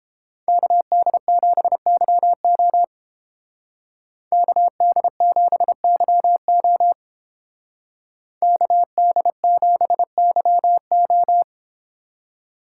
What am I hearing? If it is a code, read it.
Morse code, heard as a keyed tone, about 28 wpm, 700 Hz. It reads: KD7YO KD7YO KD7YO